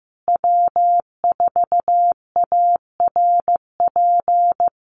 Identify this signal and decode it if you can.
Morse code, heard as a keyed tone, about 15 wpm, 700 Hz. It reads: W4ARP
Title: USA Call Signs Repeated 3x 15wpm